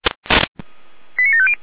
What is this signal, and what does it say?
Five-tone selective call (ZVEI), heard as quick runs of five tones, tones of about 70 ms. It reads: 89750